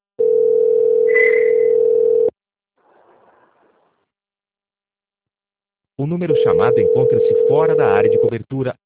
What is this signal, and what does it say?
Ringback tone (North America), 440 + 480 Hz, with speech.
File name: out- - - - - .10074